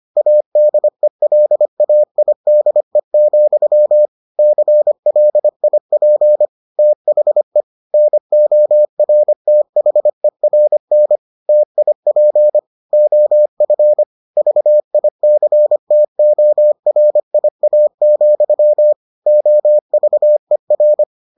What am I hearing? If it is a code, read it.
Morse code, heard as a keyed tone, about 25 wpm, 600 Hz. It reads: ADELAIDE, CLIP THE NORTHERN TIP OF VICTORIA, OVER